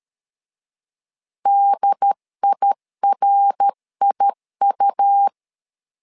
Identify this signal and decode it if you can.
Morse code, heard as a keyed tone, about 13 wpm, 800 Hz. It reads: DIRIU
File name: LEZIONI / 3 MWRO / 3 MWR 12wpm 1711